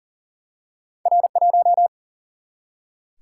Morse code, heard as a keyed tone, about 39 wpm, 700 Hz. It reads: R1
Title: Single-Letter Single-Number 40wpm